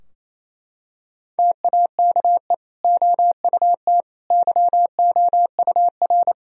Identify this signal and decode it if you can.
Morse code, heard as a keyed tone, about 28 wpm, 700 Hz. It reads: TAKE OUT YOUR
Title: Sentences from Top 1000 Words - Review Part 2 - 28wpm